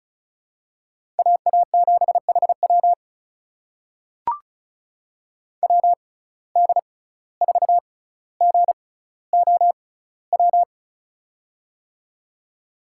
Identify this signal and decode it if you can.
Morse code, heard as a keyed tone, about 34 wpm, spaced as for 17 wpm, 700 Hz. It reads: AA7HW WD4GOW
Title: USA Call Signs - 6x Character Spacing - 35wpm